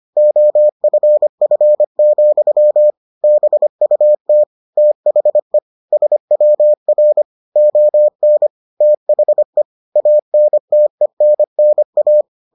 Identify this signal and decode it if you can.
Morse code, heard as a keyed tone, about 25 wpm, 600 Hz. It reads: OFF, BUT THE SWR ON THE ANTENNA